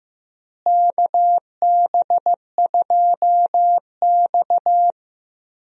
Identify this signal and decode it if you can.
Morse code, heard as a keyed tone, about 15 wpm, 700 Hz. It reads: KB2X
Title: USA Call Signs - 4 Characters - Repeated 3x - 15wpm